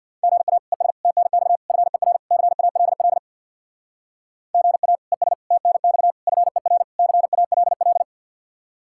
Morse code, heard as a keyed tone, about 62 wpm, 700 Hz. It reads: GA ES TNX FER CALL GA ES TNX FER CALL
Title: Speed-Racing Common QSO Phrases - 50wpm